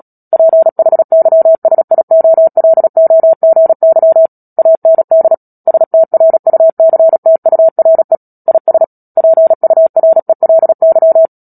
Morse code, heard as a keyed tone, about 37 wpm, 650 Hz. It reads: PHYSIOLOGY AND STRUCTURE IS PURELY
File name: Book Two / 10-Chapter 10 - Epilogue - 36 WPM